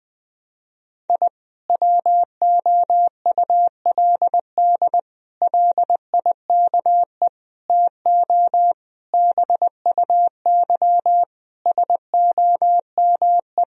Morse code, heard as a keyed tone, about 20 wpm, 700 Hz. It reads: I WOULD LIKE TO BUY SOME
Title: Sentences from Top 1000 Words - Review Part 4 - 20wpm